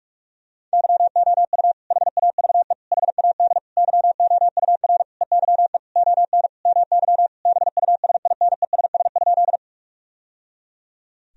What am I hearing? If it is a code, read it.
Morse code, heard as a keyed tone, about 44 wpm, 700 Hz. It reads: YOU HAVE HAD YOUR EYE ON MY BUSINESS?
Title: Sentences from Top 1000 Words - Review Part 1 - 45wpm